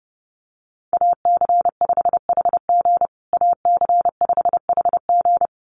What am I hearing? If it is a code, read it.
Morse code, heard as a keyed tone, about 30 wpm, 700 Hz. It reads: AC5HG AC5HG
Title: USA Call Signs Repeated 2x 30wpm